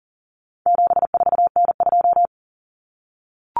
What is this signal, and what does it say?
Morse code, heard as a keyed tone, about 40 wpm, 700 Hz. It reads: 74N2
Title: Common Cut Numbers - 4-Digit 40wpm